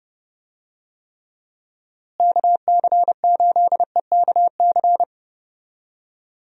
Morse code, heard as a keyed tone, about 32 wpm, 700 Hz. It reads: KC8EKC
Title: USA Call Signs - 6 Characters - Repeated 2x - 30wpm